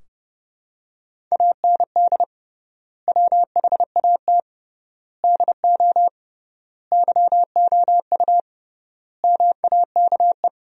Morse code, heard as a keyed tone, about 30 wpm, 700 Hz. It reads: AND WHAT DO YOU MAKE
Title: 3x Word-Spacing - Sentences from Top 100 Words - 30wpm